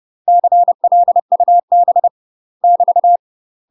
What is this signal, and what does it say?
Morse code, heard as a keyed tone, about 30 wpm, 700 Hz. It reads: CLUB <BT>
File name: UK MASTER QSOs TRAINING / 30WPM / UK MASTER QSOs 0005